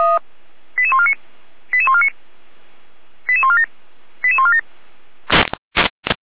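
Five-tone selective call (ZVEI), heard as quick runs of five tones, tones of about 70 ms. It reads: 88149 88149 88147 88147
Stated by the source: ZVEI five-tone call received on SDR.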